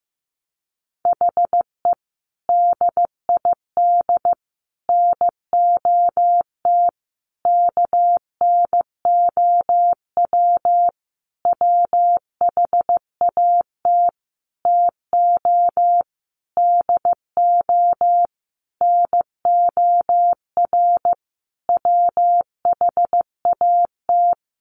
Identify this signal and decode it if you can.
Morse code, heard as a keyed tone, about 15 wpm, 700 Hz. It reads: HE DID NOT KNOW WHAT TO DO NOR WHAT
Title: Sentences from Top 1500 Words 15wpm